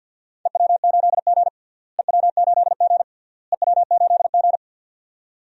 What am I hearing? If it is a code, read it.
Morse code, heard as a keyed tone, about 49 wpm, 700 Hz. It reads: EW8G EW8G EW8G
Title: DXCC-100 Call Signs - 4 Characters - Repeated 3x - 50wpm